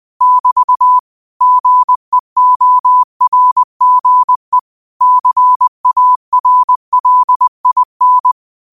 Morse code, heard as a keyed tone, about 20 wpm, 1000 Hz. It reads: <BT> GEORGE CARLIN